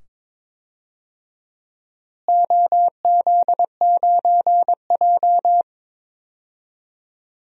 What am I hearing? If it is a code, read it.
Morse code, heard as a keyed tone, about 22 wpm, 700 Hz. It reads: OZ9J
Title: DXCC-100 Call Signs - 4 Characters - Repeated 2x - 22wpm